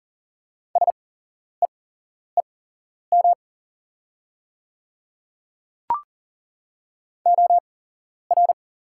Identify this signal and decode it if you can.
Morse code, heard as a keyed tone, about 40 wpm, spaced as for 11 wpm, 690 Hz. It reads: SEEM OR
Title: 8x Character Spacing - Top 300 Words - 40wpm